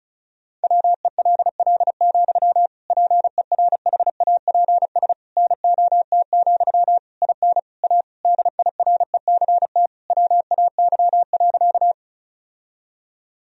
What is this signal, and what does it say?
Morse code, heard as a keyed tone, about 34 wpm, 700 Hz. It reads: WELL, PERHAPS NOT, IN A DIRECT WAY.
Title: Sentences from Top 1000 Words - Review Part 2 - 35wpm